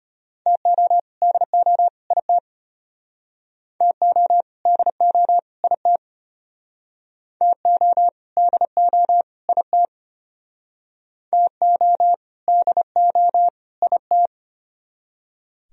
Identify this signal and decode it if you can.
Morse code, heard as a keyed tone, about 34 wpm, 700 Hz. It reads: TO DO IT TO DO IT TO DO IT TO DO IT